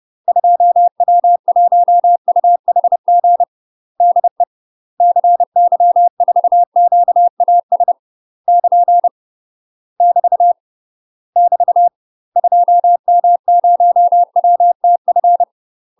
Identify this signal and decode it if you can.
Morse code, heard as a keyed tone, about 30 wpm, 700 Hz. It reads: 2W1UHG DE CY4QAS <KN> <BT> <BT> 2M0WTF
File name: UK MASTER QSOs TRAINING/30WPM